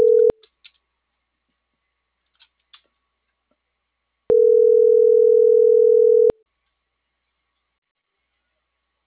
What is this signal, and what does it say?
Ringback tone (North America), 440 + 480 Hz.